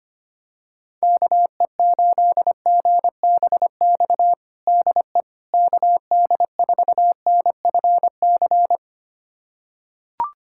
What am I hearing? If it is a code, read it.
Morse code, heard as a keyed tone, about 25 wpm, 700 Hz. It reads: KE8GBX DE KD4NFC
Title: USA Callsign DE Callsign - 6 Characters - 25wpm